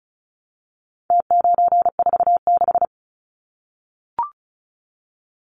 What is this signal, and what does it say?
Morse code, heard as a keyed tone, about 34 wpm, 700 Hz. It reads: T946
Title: Numbers - Contest Style - 4-Digit 35wpm